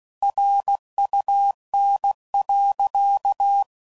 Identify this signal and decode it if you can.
Morse code, heard as a keyed tone, about 16 wpm, 800 Hz. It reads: RUN.
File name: CW Code Groups @15WPM / QSOs / WA7QOX QRX